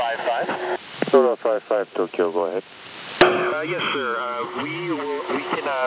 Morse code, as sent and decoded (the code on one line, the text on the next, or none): none